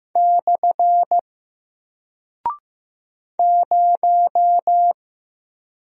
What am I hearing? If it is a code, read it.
Morse code, heard as a keyed tone, about 15 wpm, 700 Hz. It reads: / 0
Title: Single Character (Letter:Number:Punctuation:Prosign) Alt 15wpm